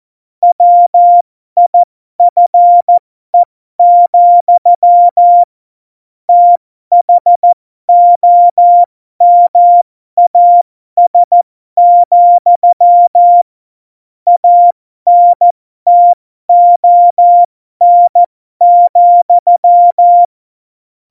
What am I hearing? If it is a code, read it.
Morse code, heard as a keyed tone, about 14 wpm, spaced as for 12 wpm, 700 Hz. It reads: WIFE, THOMAS, ANTON,
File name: Morsekurs / Witze